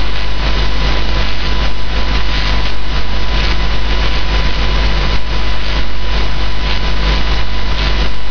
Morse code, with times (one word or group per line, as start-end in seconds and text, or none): none